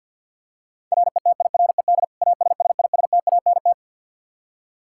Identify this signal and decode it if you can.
Morse code, heard as a keyed tone, about 49 wpm, 700 Hz. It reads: RETIRED ASSISTANT
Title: Practice QSO - Exchange 3 - More Info - 50wpm